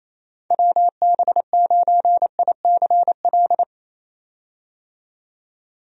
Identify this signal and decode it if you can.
Morse code, heard as a keyed tone, about 27 wpm, 700 Hz. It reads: WB9ICL
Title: USA Call Signs - 6 Characters - 28wpm